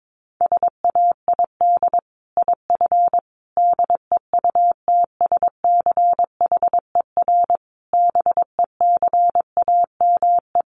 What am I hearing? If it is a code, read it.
Morse code, heard as a keyed tone, about 22 wpm, 700 Hz. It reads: SAID IF DEUTSCHER BECAME
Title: A Sound Of Thunder 22wpm